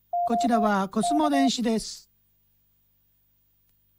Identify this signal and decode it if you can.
Morse code, heard as a keyed tone, about 24 wpm, spaced as for 16 wpm, 690 Hz, with speech. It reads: OK